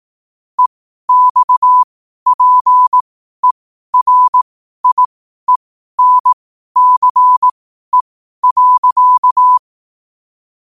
Morse code, heard as a keyed tone, about 18 wpm, spaced as for 12 wpm, 1000 Hz. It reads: EXPERIENCE.
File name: qotdfiles / 12wpm